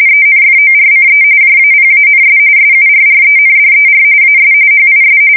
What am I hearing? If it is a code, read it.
RTTY, about 45.45 baud. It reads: ENFORCE JAPANS SURRENDER AS HIS
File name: fsk170 / USS-New-Jersey